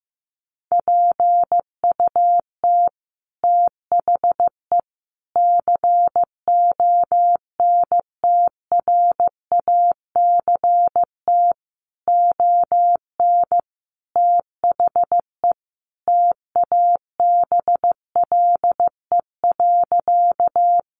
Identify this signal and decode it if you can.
Morse code, heard as a keyed tone, about 15 wpm, 700 Hz. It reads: PUT THE CONTRACT ON THE TABLE.
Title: Sentences from Top 1400 Words 15wpm